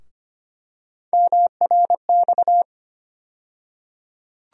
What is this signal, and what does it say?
Morse code, heard as a keyed tone, about 25 wpm, 700 Hz. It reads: MRX
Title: USA Call Sign Postfixes - 3 Letters 25wpm